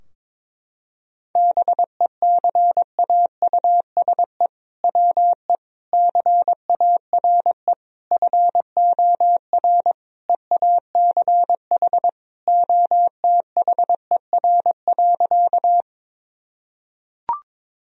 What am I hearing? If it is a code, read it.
Morse code, heard as a keyed tone, about 22 wpm, 700 Hz. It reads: BECAUSE WE CARE FOR EACH OTHER.
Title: Sentences from Top 1000 Words - Review Part 3 - 22wpm